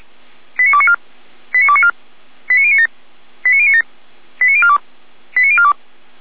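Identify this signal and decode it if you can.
Five-tone selective call (ZVEI), heard as quick runs of five tones, tones of about 70 ms. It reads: 89294 89294 89097 89097 89042 89042
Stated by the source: ZVEI five-tone selective call received by SDR, from audio.f85360.feuerwehr-orlamuende.org fire brigade server.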